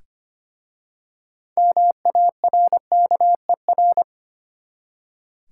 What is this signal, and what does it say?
Morse code, heard as a keyed tone, about 25 wpm, 700 Hz. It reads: MARKER
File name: practice / 25wpm / Easy Words with Hints - 25wpm